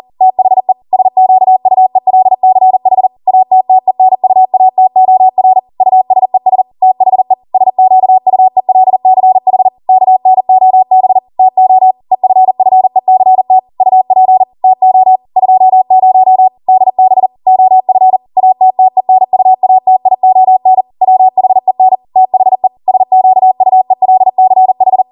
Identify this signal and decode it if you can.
Morse code, heard as a keyed tone, about 40 wpm, 750 Hz. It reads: THE SQUELCH ATTENUATOR USES THE SQUELCH KNOB TO EFFECT UP TO 20 DB OF ATTENUATION WHEN THE SQUELCH
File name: Morse / 40WPM